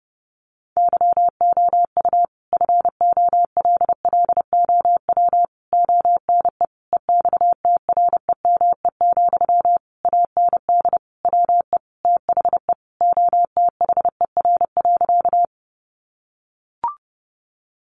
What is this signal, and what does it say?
Morse code, heard as a keyed tone, about 30 wpm, 700 Hz. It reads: YOU FOLLOW ONE EXTREME, AND WE THE OTHER.
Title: Sentences from Top 1300 Words 30wpm